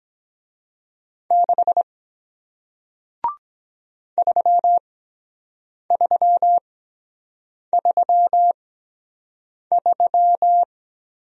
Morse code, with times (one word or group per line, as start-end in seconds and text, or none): 1.3-1.8 6
4.2-10.6 3333